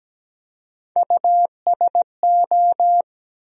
Morse code, heard as a keyed tone, about 17 wpm, 700 Hz. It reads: USO
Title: USA Call Sign Postfixes - 3 Letters 17wpm